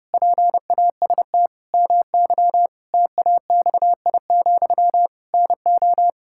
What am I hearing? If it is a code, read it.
Morse code, heard as a keyed tone, about 30 wpm, 700 Hz. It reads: PAST MY TAXI, NO
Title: Pangram Challenge v2 - 30wpm